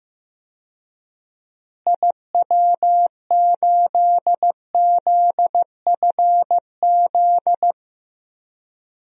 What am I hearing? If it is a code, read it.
Morse code, heard as a keyed tone, about 15 wpm, 700 Hz. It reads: IW8ZFZ
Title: DXCC-100 Call Signs - 6 Characters - Repeated 3x - 15wpm